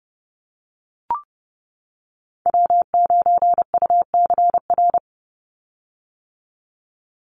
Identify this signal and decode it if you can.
Morse code, heard as a keyed tone, about 30 wpm, 700 Hz. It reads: W9UCR
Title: Call Signs - 30wpm